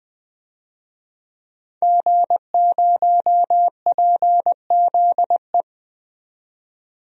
Morse code, heard as a keyed tone, about 20 wpm, 700 Hz. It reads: G0PZE
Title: DXCC-100 Call Signs - 5 Characters - 20wpm